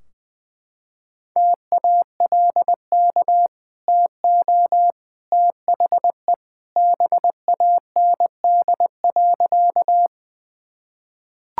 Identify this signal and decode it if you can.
Morse code, heard as a keyed tone, about 20 wpm, 700 Hz. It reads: TALK TO THE BAND.
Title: Sentences from Top 1600 Words 20wpm